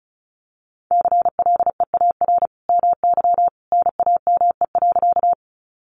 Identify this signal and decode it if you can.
Morse code, heard as a keyed tone, about 34 wpm, 700 Hz. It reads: CLEAR MY NAME.
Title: Sentences from Top 500 Words 35wpm